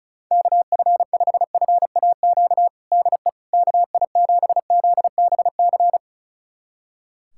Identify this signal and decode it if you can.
Morse code, heard as a keyed tone, about 36 wpm, 700 Hz. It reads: KF5FAQ DE KI7ZBC
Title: USA Callsign DE Callsign 35wpm